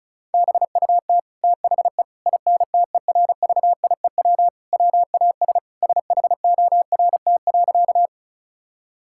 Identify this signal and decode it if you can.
Morse code, heard as a keyed tone, about 36 wpm, 700 Hz. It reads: BUT THE INTERVIEW WAS SHORT.
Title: Sentences from Top 1600 Words 35wpm